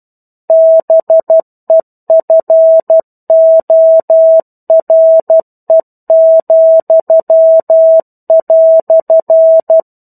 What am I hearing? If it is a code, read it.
Morse code, heard as a keyed tone, about 12 wpm, 650 Hz. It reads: BEFORE,"
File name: 01-Chapter-01 12 WPM